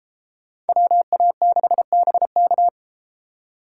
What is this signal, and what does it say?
Morse code, heard as a keyed tone, about 31 wpm, 700 Hz. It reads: WA6BK